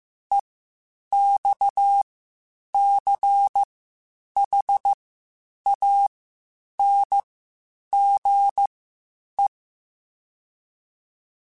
Morse code, heard as a keyed tone, about 15 wpm, spaced as for 8 wpm, 800 Hz. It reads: EXCHANGE
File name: (spaced) / QSOs Spaced-mp3s / WA7QOX 1 @13 sp